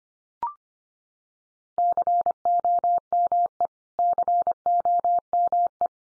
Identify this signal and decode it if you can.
Morse code, heard as a keyed tone, about 25 wpm, 700 Hz. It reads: COME COME
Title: Top 100 Words 25wpm x3